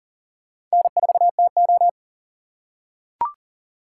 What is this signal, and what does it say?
Morse code, heard as a keyed tone, about 40 wpm, 700 Hz. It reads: N4TO
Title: USA Call Signs - 4 Characters - 3x Character Spacing - 40wpm